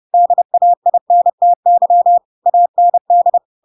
Morse code, heard as a keyed tone, about 30 wpm, 700 Hz. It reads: DAINTY AND
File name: Pinocchio / Pinocchio ch7 30 30 ik1hge